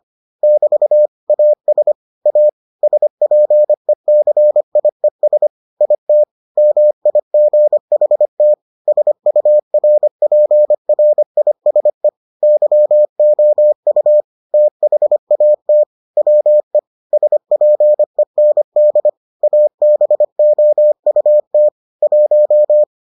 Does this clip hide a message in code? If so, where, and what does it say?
Yes: Morse code from 0.4 to 22.8 s, heard as a keyed tone, about 25 wpm, 600 Hz. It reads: <BT> AS A SPECIES IT MIGHT SURPRISE YOU THAT WE SPEND ABOUT 1